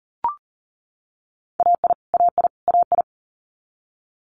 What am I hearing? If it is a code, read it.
Morse code, heard as a keyed tone, about 38 wpm, 700 Hz. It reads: AI AI AI
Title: Multi-letter Phonemes Repeated 3x 40wpm